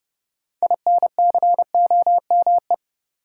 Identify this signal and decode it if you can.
Morse code, heard as a keyed tone, about 30 wpm, 700 Hz. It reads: INCOME
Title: Sentences from Top 1000 Words - Review Part 5 - New Word Drill - 30wpm